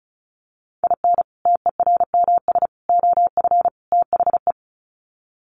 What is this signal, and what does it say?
Morse code, heard as a keyed tone, about 34 wpm, 700 Hz. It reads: IN TERMS OF THE
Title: Sets of 4 Words 35wpm